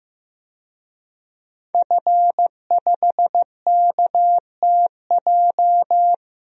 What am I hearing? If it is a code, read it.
Morse code, heard as a keyed tone, about 15 wpm, 700 Hz. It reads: F5KTJ
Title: DXCC-100 Call Signs - 5 Characters - Repeated 3x - 15wpm